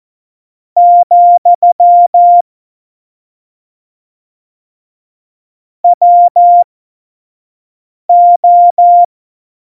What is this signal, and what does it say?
Morse code, heard as a keyed tone, about 14 wpm, spaced as for 5 wpm, 700 Hz. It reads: , WO
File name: Morsekurs / Witze / Witz 13 March 2018, 25 F70